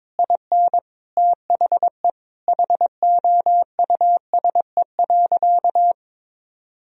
Morse code, heard as a keyed tone, about 22 wpm, 700 Hz. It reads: IN THE HOUSE.
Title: Sentences from Top 500 Words and Abbreviations 22wpm